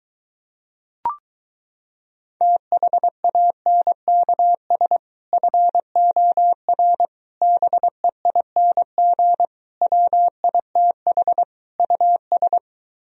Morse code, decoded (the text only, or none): THANKS FOR BEING WITH US